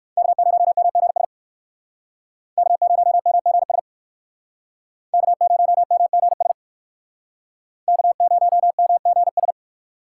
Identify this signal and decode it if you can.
Morse code, heard as a keyed tone, about 57 wpm, 700 Hz. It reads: K0MGS K0MGS K0MGS K0MGS